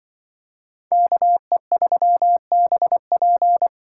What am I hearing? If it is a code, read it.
Morse code, heard as a keyed tone, about 25 wpm, 700 Hz. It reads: KE3BP
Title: Speed Racing - USA 2x2 Call Signs 20wpm